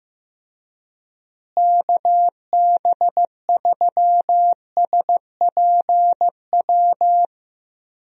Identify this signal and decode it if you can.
Morse code, heard as a keyed tone, about 15 wpm, 700 Hz. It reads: KB3SPW
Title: USA Call Signs - 6 Characters - Repeated 3x - 15wpm